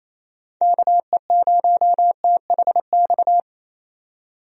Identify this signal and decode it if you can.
Morse code, heard as a keyed tone, about 27 wpm, 700 Hz. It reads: KE0THX
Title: USA Call Signs - 6 Characters - 28wpm